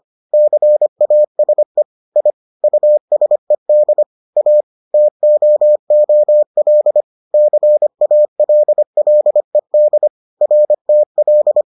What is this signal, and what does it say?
Morse code, heard as a keyed tone, about 25 wpm, 600 Hz. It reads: CASE I USED A TOOL CALLED RTL